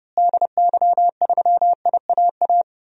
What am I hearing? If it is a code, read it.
Morse code, heard as a keyed tone, about 30 wpm, 700 Hz. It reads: DY3IAA
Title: DXCC-100 Call Signs - 6 Characters - Repeated 3x - 30wpm